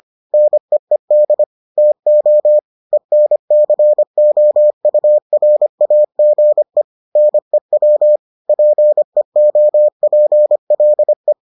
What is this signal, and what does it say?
Morse code, heard as a keyed tone, about 25 wpm, 600 Hz. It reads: NEED TO ENCOURAGE NEW PEOPLE